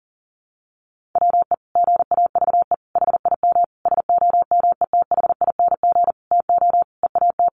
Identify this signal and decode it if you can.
Morse code, heard as a keyed tone, about 40 wpm, 700 Hz. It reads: WE GAVE HIM SOMETHING TO EAT
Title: Sentences from Top 600 Words 40wpm